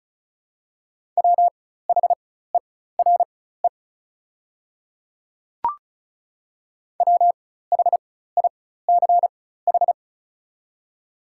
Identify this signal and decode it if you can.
Morse code, heard as a keyed tone, about 36 wpm, spaced as for 16 wpm, 700 Hz. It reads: WHERE WHICH
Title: 4x Character Spacing - Top 200 Words - 35wpm